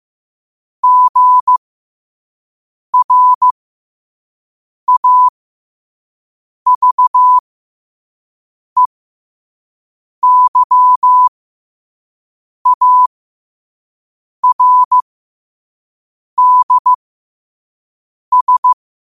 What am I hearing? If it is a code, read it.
Morse code, heard as a keyed tone, about 15 wpm, spaced as for 5 wpm, 1000 Hz. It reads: GRAVEYARDS
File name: qotdfiles / 5wpm